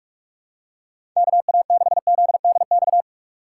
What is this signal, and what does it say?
Morse code, heard as a keyed tone, about 46 wpm, 700 Hz. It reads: KA6ZDX
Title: USA Call Signs - 6 Characters - 2x Character Spacing - 45wpm